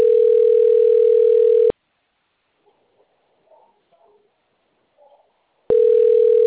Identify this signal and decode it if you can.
Ringback tone (North America), 440 + 480 Hz.
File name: rg-1600-unknown- - - .85606